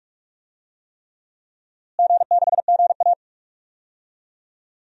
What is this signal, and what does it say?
Morse code, heard as a keyed tone, about 46 wpm, 700 Hz. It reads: G6GA
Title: DXCC-100 Call Signs - 4 Characters - Repeated 3x - 45wpm